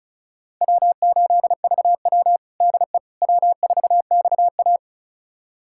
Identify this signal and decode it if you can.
Morse code, heard as a keyed tone, about 36 wpm, 700 Hz. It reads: W8VW DE W4XA
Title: USA Callsign DE Callsign - 4 Characters - 35wpm